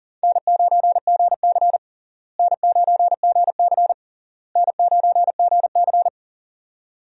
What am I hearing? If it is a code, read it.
Morse code, heard as a keyed tone, about 37 wpm, 700 Hz. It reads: N9GC N9GC N9GC
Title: USA Call Signs - 4 Characters - Repeated 3x - 40wpm